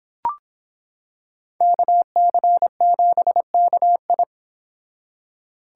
Morse code, heard as a keyed tone, about 26 wpm, 700 Hz. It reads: KC7KI